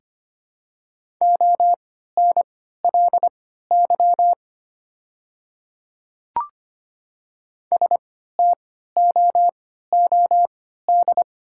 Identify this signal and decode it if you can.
Morse code, heard as a keyed tone, about 25 wpm, spaced as for 14 wpm, 690 Hz. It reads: ONLY STOOD
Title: 3x Character Spacing - Top 500 Words - 25wpm